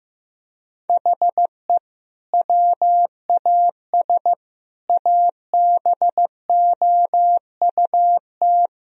Morse code, heard as a keyed tone, about 15 wpm, 700 Hz. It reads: HE WAS ABOUT
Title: Sentences from Top 1000 Words - Review Part 2 - 15wpm